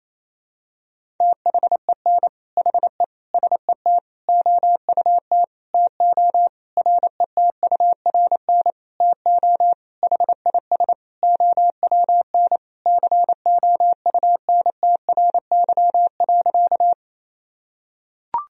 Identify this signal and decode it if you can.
Morse code, heard as a keyed tone, about 28 wpm, 700 Hz. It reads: THEN HE SET OUT TO RETURN TO HIS OWN COUNTRY.
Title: Sentences from Top 1000 Words - Review Part 1 - 28wpm